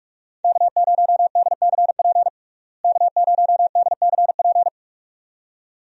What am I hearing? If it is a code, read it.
Morse code, heard as a keyed tone, about 46 wpm, 700 Hz. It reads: K0DCP K0DCP
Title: USA Call Signs - 5 Characters - Repeated 2x - 45wpm